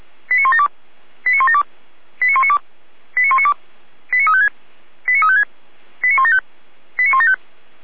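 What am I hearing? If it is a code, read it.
Five-tone selective call (ZVEI), heard as quick runs of five tones, tones of about 70 ms. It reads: 89172 89172 89192 89192 89356 89356 89175 89175
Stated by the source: ZVEI five-tone selective call, code 89560, received via SDR.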